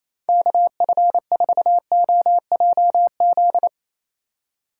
Morse code, heard as a keyed tone, about 35 wpm, 700 Hz. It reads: KF4OJZ
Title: Speed Racing - USA Call Signs 28wpm